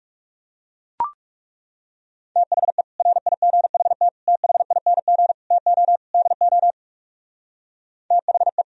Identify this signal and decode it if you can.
Morse code, heard as a keyed tone, about 37 wpm, 700 Hz. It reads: THE RIGHT THING TO DO THE
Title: Speed-Racing - Sets of 5 Words 30wpm